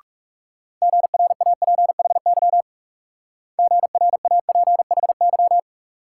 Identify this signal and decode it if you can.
Morse code, heard as a keyed tone, about 38 wpm, 700 Hz. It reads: GRAPHY GRAPHY